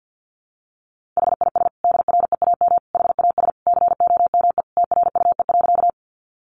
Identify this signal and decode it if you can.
Morse code, heard as a keyed tone, about 50 wpm, 700 Hz. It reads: HIS DREAM HAS COME TRUE.
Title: Sentences from Top 1500 Words 50wpm